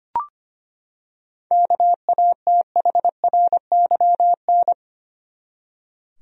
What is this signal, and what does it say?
Morse code, heard as a keyed tone, about 25 wpm, 700 Hz. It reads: KATHRYN
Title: USA Common Names - 25wpm